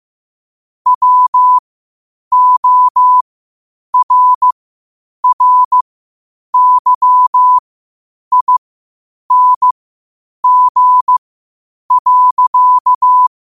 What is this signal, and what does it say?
Morse code, heard as a keyed tone, about 15 wpm, spaced as for 8 wpm, 1000 Hz. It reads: WORRYING.
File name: qotdfiles / 8wpm